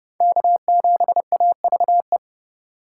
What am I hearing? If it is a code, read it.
Morse code, heard as a keyed tone, about 29 wpm, 700 Hz. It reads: K7AVE